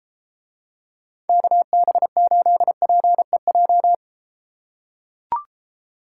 Morse code, heard as a keyed tone, about 34 wpm, 700 Hz. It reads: KB8PEJ